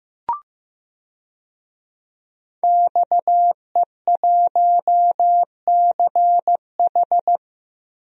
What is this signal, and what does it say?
Morse code, heard as a keyed tone, about 15 wpm, 700 Hz. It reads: XE1CH
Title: DXCC-100 Call Signs - 5 Characters - 15wpm